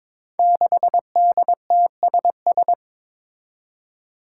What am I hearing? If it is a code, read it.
Morse code, heard as a keyed tone, about 22 wpm, 700 Hz. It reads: 6DTSS
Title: Five Challenging Letter-Number Combinations 22wpm